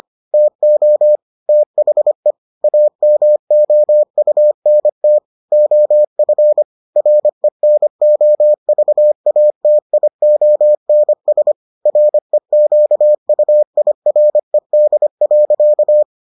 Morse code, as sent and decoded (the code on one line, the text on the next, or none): - --- / - .... / .- -- --- ..- -. - / --- ..-. / .-. -. --- ...- .- - .. --- -. ... / .-. --.- ..- .. .-. -.. .-.-.-
TO THE AMOUNT OF RENOVATIONS REQUIRED.